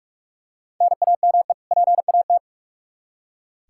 Morse code, heard as a keyed tone, about 46 wpm, 700 Hz. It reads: NAME PAT
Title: Practice QSO - Exchange 1 - Beginning - 45wpm